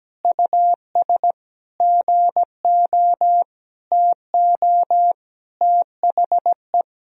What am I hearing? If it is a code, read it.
Morse code, heard as a keyed tone, about 17 wpm, 700 Hz. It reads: US GO TO THE
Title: Sentences from Top 1000 Words - Review Part 4 - 17wpm